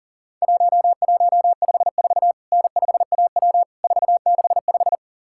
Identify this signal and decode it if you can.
Morse code, heard as a keyed tone, about 40 wpm, 700 Hz. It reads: 1154 N5AW 465
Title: Worked All Europe DX Contest - QTC Exchange - 40wpm